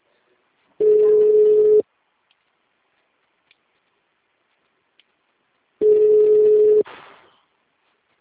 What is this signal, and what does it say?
Ringback tone (Europe, 425 Hz).
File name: out- -2031- - - .40336